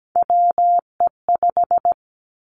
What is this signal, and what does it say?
Morse code, heard as a keyed tone, about 17 wpm, 700 Hz. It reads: WE5
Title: Three Challenging Letter-Number Combinations 17wpm